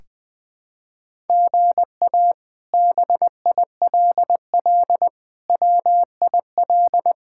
Morse code, heard as a keyed tone, about 21 wpm, 700 Hz. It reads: GA BILL WIL
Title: K1USN SST QSO Practice - 20wpm